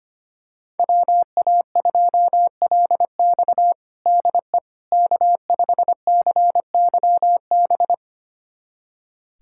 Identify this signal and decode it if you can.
Morse code, heard as a keyed tone, about 25 wpm, 700 Hz. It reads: WA2LX DE K5CYB
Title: USA Callsign DE Callsign - 5 Characters - 25wpm